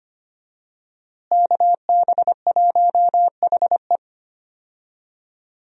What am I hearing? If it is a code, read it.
Morse code, heard as a keyed tone, about 25 wpm, 700 Hz. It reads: KB1HE
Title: USA Call Signs - 5 Characters - Repeated 3x - 25wpm